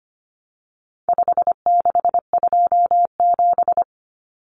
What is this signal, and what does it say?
Morse code, heard as a keyed tone, about 25 wpm, 700 Hz. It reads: 5627
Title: Common Cut Numbers - 4-Digit 25wpm